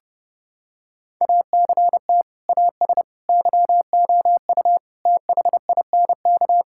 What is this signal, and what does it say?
Morse code, heard as a keyed tone, about 30 wpm, 700 Hz. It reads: ACT AS YOU THINK
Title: Sentences from Top 1000 Words - Review Part 5 - 30wpm